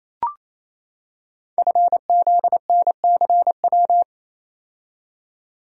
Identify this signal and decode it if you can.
Morse code, heard as a keyed tone, about 27 wpm, 700 Hz. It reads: FZNCW
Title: Code Groups - 5 Letters - 28wpm